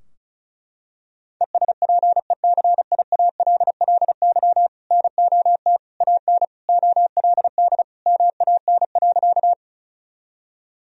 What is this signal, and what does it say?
Morse code, heard as a keyed tone, about 36 wpm, 700 Hz. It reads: ESPECIALLY NOT AN OLD MAN.
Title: Sentences from Top 1000 Words - Review Part 4 - 35wpm